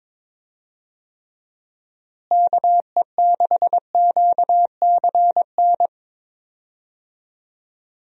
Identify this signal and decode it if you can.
Morse code, heard as a keyed tone, about 21 wpm, 700 Hz. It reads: KE6QCN
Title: USA Call Signs - 6 Characters - Repeated 2x - 22wpm